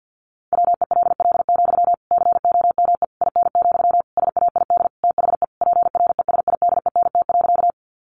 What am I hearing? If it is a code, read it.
Morse code, heard as a keyed tone, about 50 wpm, 700 Hz. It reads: WELL, COME IN, SAID THE PRESIDENT.
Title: Sentences from Top 1000 Words 50wpm